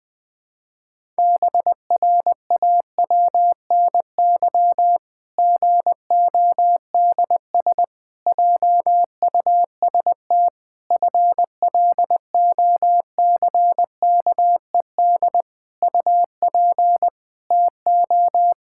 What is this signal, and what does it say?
Morse code, heard as a keyed tone, about 20 wpm, 700 Hz. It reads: BRAWNY GODS JUST FLOCKED UP TO
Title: Pangram Challenge v2 - 20wpm